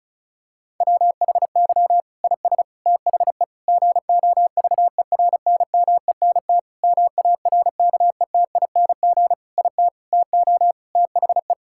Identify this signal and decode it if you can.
Morse code, heard as a keyed tone, about 36 wpm, 700 Hz. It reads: WHY IS THE GOVERNMENT MARKETING IT TO THE
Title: Sentences from Top 1000 Words - Review Part 5 - 35wpm